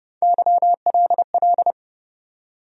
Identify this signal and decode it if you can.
Morse code, heard as a keyed tone, about 30 wpm, 700 Hz. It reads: YLL
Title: USA Call Sign Postfixes - 3 Letters 30wpm